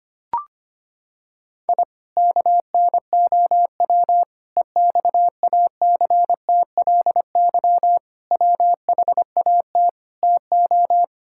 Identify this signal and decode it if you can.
Morse code, heard as a keyed tone, about 25 wpm, 700 Hz. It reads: I KNOW EXACTLY WHAT TO
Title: Sentences from Top 1000 Words - Review Part 6 - 25wpm